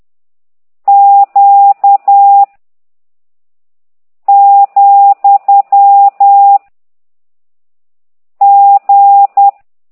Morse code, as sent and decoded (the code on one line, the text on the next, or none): --.- / --..-- / --.
Q , G